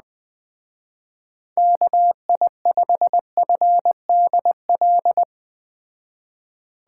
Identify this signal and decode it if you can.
Morse code, heard as a keyed tone, about 19 wpm, 700 Hz. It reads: KI5FDL
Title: Speed Racing - USA 2x3 Call Signs 17wpm